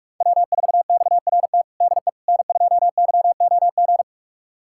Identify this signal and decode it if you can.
Morse code, heard as a keyed tone, about 46 wpm, 690 Hz. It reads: W4XRT DE N2YOG
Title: USA Callsign DE Callsign 45wpm